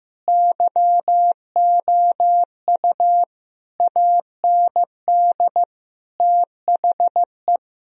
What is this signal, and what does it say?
Morse code, heard as a keyed tone, about 16 wpm, 700 Hz. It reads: YOU AND THE
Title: Sentences from Top 1000 Words - Review Part 4 - 15wpm